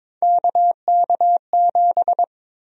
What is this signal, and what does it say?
Morse code, heard as a keyed tone, about 22 wpm, 700 Hz. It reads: KK7
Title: USA Call Sign Prefixes - 22wpm